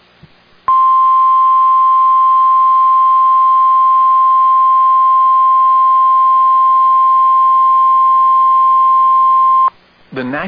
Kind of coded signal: Emergency Alert System (SAME) tones